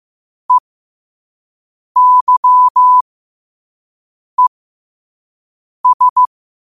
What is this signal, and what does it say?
Morse code, heard as a keyed tone, about 15 wpm, spaced as for 5 wpm, 1000 Hz. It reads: EYES